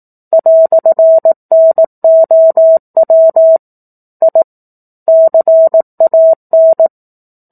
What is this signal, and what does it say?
Morse code, heard as a keyed tone, about 18 wpm, 650 Hz. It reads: "NOW I CAN
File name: 04-Chapter-04 18 WPM